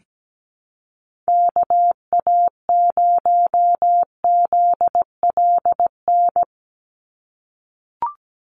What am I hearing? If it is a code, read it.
Morse code, heard as a keyed tone, about 17 wpm, 700 Hz. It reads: KA0ZLN
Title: USA Call Signs 17wpm